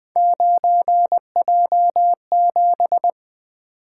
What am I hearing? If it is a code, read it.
Morse code, heard as a keyed tone, about 20 wpm, 700 Hz. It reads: 9J7
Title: Three Letter-Number Combinations - 20wpm